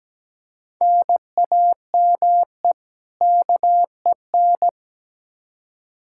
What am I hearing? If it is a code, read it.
Morse code, heard as a keyed tone, about 17 wpm, 700 Hz. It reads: NAME KEN
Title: Practice QSO - Exchange 1 - Beginning - 17wpm